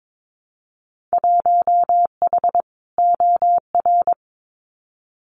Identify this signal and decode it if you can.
Morse code, heard as a keyed tone, about 22 wpm, 700 Hz. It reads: 1H OR
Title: Winter Field Day Exchanges 22wpm